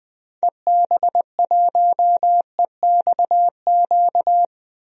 Morse code, heard as a keyed tone, about 21 wpm, 700 Hz. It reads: EB1EXQ
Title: DXCC-100 Call Signs - 6 Characters - 20wpm